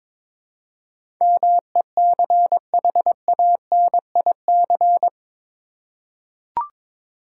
Morse code, heard as a keyed tone, about 21 wpm, 700 Hz. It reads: MECHANIC
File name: practice / 22wpm / Eight Letter Words - 22wpm